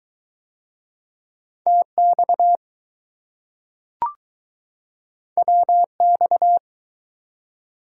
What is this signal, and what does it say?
Morse code, heard as a keyed tone, about 20 wpm, 700 Hz. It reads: TX WX